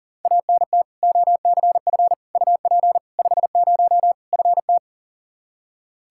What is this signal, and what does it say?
Morse code, heard as a keyed tone, about 40 wpm, 700 Hz. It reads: ANT OCF UP 50 FT
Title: Practice QSO - Exchange 2 - Info - 40wpm